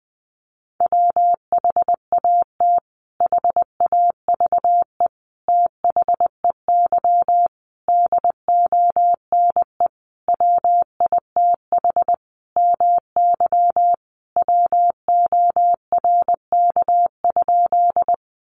Morse code, heard as a keyed tone, about 20 wpm, 700 Hz. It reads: WHAT HAVE THEY DONE WITH MY WORK?
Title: Sentences from Top 500 Words 20wpm